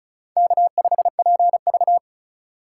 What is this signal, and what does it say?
Morse code, heard as a keyed tone, about 44 wpm, 700 Hz. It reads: K5PV
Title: Speed Racing - USA 1x2 Call Signs 35wpm